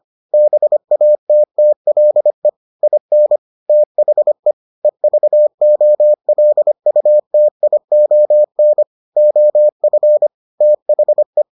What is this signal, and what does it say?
Morse code, heard as a keyed tone, about 25 wpm, 600 Hz. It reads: BATTLE IN THE EVOLUTION OF THE